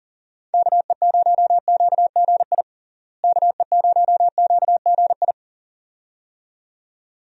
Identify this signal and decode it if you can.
Morse code, heard as a keyed tone, about 40 wpm, 700 Hz. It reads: KE0QGI KE0QGI
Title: USA Call Signs - 6 Characters - Repeated 2x - 40wpm